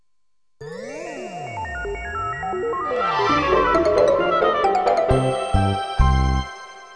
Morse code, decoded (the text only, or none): none